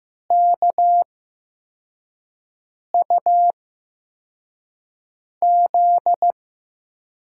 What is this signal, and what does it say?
Morse code, heard as a keyed tone, about 16 wpm, spaced as for 4 wpm, 700 Hz. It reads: KUZ
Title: USA Call Signs - 6 Characters - 8x Character Spacing - 15wpm